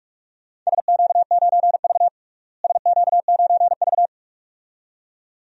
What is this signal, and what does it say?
Morse code, heard as a keyed tone, about 42 wpm, 700 Hz. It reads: SQ9V SQ9V
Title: DXCC-100 Call Signs - 4 Characters - Repeated 2x - 45wpm